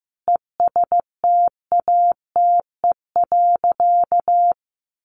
Morse code, heard as a keyed tone, about 15 wpm, 700 Hz. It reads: ESTATE.
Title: Sentences from Top 1200 Words 15wpm